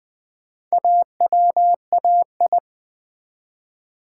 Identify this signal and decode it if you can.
Morse code, heard as a keyed tone, about 20 wpm, 700 Hz. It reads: AWAI
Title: Build A Word - 1 Letter At A Time - 5-Letter Words - 20wpm